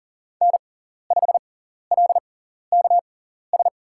Morse code, heard as a keyed tone, about 40 wpm, spaced as for 14 wpm, 700 Hz. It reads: N5LKS
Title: USA Call Signs - 5 Characters - 6x Character Spacing - 40wpm